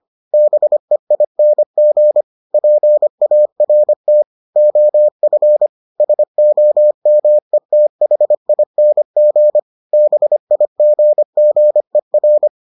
Morse code, heard as a keyed tone, about 25 wpm, 600 Hz. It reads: BEING PART OF SOMETHING BIGGER